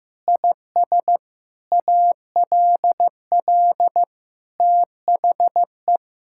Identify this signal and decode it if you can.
Morse code, heard as a keyed tone, about 15 wpm, 700 Hz. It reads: IS ALL THE
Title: Sentences from Top 1000 Words - Review Part 5 - 15wpm